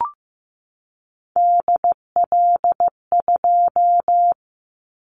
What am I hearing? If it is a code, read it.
Morse code, heard as a keyed tone, about 15 wpm, 700 Hz. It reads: DL2
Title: Callsign Prefixes for Top 50 DXCC - 15wpm